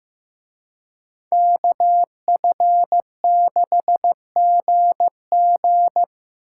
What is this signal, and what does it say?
Morse code, heard as a keyed tone, about 18 wpm, 700 Hz. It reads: KF6GG
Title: Speed Racing - USA Call Signs 15wpm